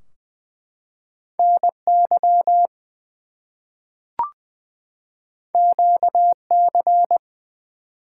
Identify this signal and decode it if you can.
Morse code, heard as a keyed tone, about 20 wpm, 700 Hz. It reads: NY QC
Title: US States and Canadian Provinces 20wpm